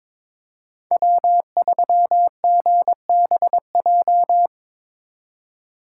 Morse code, heard as a keyed tone, about 27 wpm, 700 Hz. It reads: W3GBJ
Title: Speed Racing - USA 1x3 Call Signs 22wpm